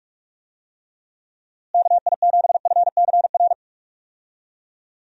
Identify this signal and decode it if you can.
Morse code, heard as a keyed tone, about 46 wpm, 700 Hz. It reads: KI7FCR
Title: USA Call Signs - 6 Characters - Repeated 2x - 45wpm